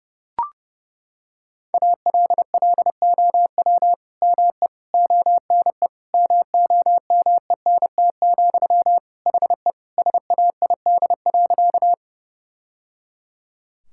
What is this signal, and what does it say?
Morse code, heard as a keyed tone, about 32 wpm, 700 Hz. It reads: ALLOW ME ONE MOMENT, HE SAID.
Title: Sentences from Top 1000 Words - Review Part 5 - 30wpm